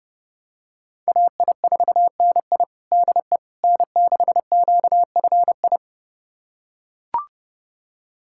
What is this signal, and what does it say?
Morse code, heard as a keyed tone, about 30 wpm, 700 Hz. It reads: AI4NI DE N6QFI
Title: USA Callsign DE Callsign - 5 Characters - 30wpm